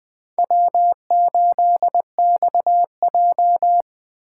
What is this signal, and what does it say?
Morse code, heard as a keyed tone, about 19 wpm, 700 Hz. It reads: W8XJ